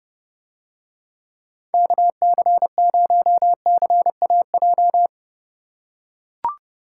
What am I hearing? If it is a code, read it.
Morse code, heard as a keyed tone, about 30 wpm, 700 Hz. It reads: KC0CAJ